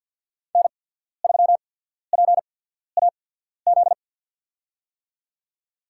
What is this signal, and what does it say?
Morse code, heard as a keyed tone, about 53 wpm, spaced as for 14 wpm, 700 Hz. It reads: N3PAZ
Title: USA Call Signs - 5 Characters - 8x Character Spacing - 50wpm